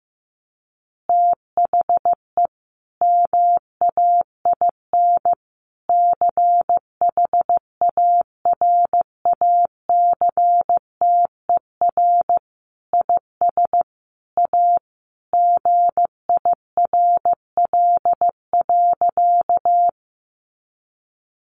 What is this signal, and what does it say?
Morse code, heard as a keyed tone, about 15 wpm, 700 Hz. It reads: THE MAIN CHARACTER IS A GIRL.
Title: Additional Easy Phrases 15wpm